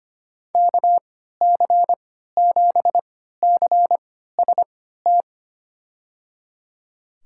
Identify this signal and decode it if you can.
Morse code, heard as a keyed tone, about 25 wpm, spaced as for 14 wpm, 700 Hz. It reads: KC7CST
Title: USA Call Signs - 6 Characters - 3x Character Spacing - 25wpm